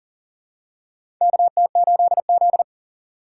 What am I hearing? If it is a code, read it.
Morse code, heard as a keyed tone, about 38 wpm, 700 Hz. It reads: KT8Z
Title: USA Call Signs - 4 Characters - 6x Character Spacing - 40wpm